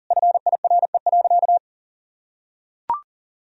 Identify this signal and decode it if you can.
Morse code, heard as a keyed tone, about 37 wpm, 700 Hz. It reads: FIRE.
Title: Sentences from Top 1000 Words - Review Part 1 - 40wpm